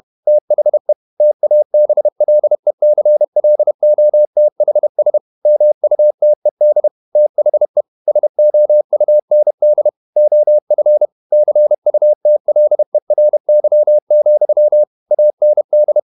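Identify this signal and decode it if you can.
Morse code, heard as a keyed tone, about 31 wpm, 600 Hz. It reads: THE TABLECLOTHS MUTED THE SOUND OF CUTLERY, AND
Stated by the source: Morse code at 30 wpm, the novel Never by Ken Follett in English, from CW med Gnister.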